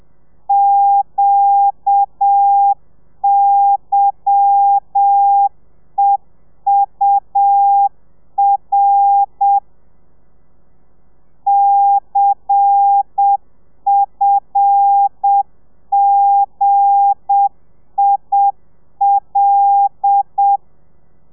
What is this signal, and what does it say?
Morse code, heard as a keyed tone, about 7 wpm, 800 Hz. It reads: QYEUR CFGIL
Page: Lezione 1: audio a 30 CPM